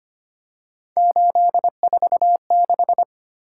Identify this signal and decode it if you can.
Morse code, heard as a keyed tone, about 25 wpm, 700 Hz. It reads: 846
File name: Three Letter-Number Combinations - 25wpm